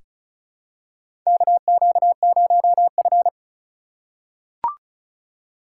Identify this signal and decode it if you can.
Morse code, heard as a keyed tone, about 36 wpm, 700 Hz. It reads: KQ0F
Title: USA Call Signs - 4 Characters - 3x Character Spacing - 35wpm